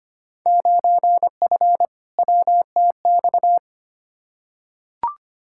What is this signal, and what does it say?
Morse code, heard as a keyed tone, about 25 wpm, 700 Hz. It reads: 9F WTX
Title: Field Day Exchanges Encore - 25wpm